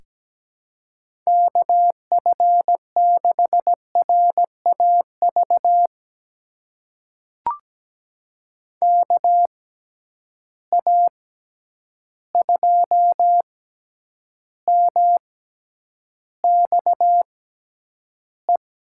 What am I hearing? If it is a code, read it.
Morse code, heard as a keyed tone, about 17 wpm, spaced as for 9 wpm, 700 Hz. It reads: KF6RAV KA2MXE
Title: USA Call Signs - 6x Character Spacing - 17wpm